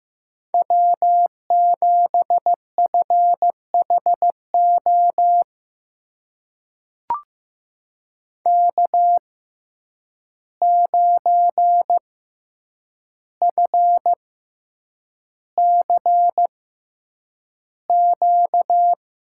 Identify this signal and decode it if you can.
Morse code, heard as a keyed tone, about 15 wpm, spaced as for 8 wpm, 700 Hz. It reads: W7FHO K9FCQ
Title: USA Call Signs - 5 Characters - 6x Character Spacing - 15wpm